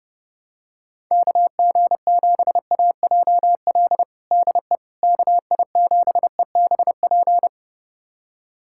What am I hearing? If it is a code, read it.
Morse code, heard as a keyed tone, about 32 wpm, 700 Hz. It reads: KG7AJL DE KI7EBP
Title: USA Callsign DE Callsign - 6 Characters - 30wpm